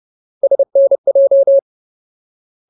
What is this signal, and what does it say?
Morse code, heard as a keyed tone, about 30 wpm, 550 Hz. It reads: SNJ
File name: Ultimate Word List / 30WPM Ultimate